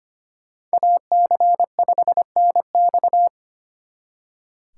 Morse code, heard as a keyed tone, about 33 wpm, 700 Hz. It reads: AC5NX